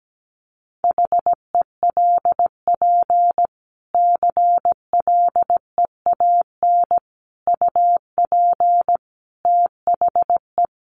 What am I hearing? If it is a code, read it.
Morse code, heard as a keyed tone, about 17 wpm, 700 Hz. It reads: HELP CLEAN UP THE
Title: Sentences from Top 1100 Words 17wpm